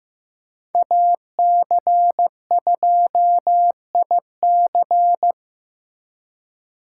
Morse code, heard as a keyed tone, about 18 wpm, 700 Hz. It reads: AC2IC
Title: Speed Racing - USA 2x2 Call Signs 15wpm